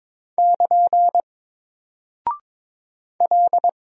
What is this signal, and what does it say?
Morse code, heard as a keyed tone, about 22 wpm, 700 Hz. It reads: <KN> L
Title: Single Character (Letter:Number:Punctuation:Prosign) Alt 22wpm